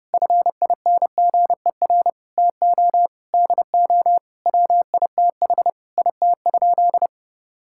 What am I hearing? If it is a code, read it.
Morse code, heard as a keyed tone, about 30 wpm, 700 Hz. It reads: FINGER TO DO WITH IT?
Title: Sentences from Top 1600 Words 30wpm